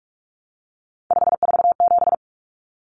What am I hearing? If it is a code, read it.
Morse code, heard as a keyed tone, about 44 wpm, 700 Hz. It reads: 547
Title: Numbers - 3-Digit 45wpm